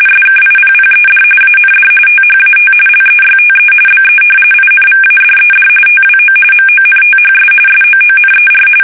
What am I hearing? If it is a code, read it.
RTTY, about 45.45 baud. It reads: SKIPPED AND THAT LEYTE BE INVADED TWO MONTHS SOONER
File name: fsk850 / battleshipnj